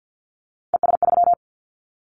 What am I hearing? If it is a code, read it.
Morse code, heard as a keyed tone, about 50 wpm, 700 Hz. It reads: ES3
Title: Callsign Prefixes for Top 50 DXCC - 50wpm